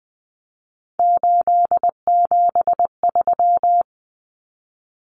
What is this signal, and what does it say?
Morse code, heard as a keyed tone, about 20 wpm, 700 Hz. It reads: 873
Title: Numbers - 3-Digit 20wpm